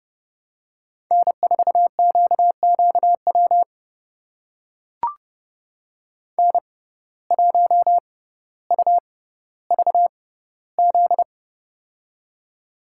Morse code, heard as a keyed tone, about 30 wpm, spaced as for 15 wpm, 700 Hz. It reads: N4QQW N1UVZ
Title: USA Call Signs - 5 Characters - 6x Character Spacing - 30wpm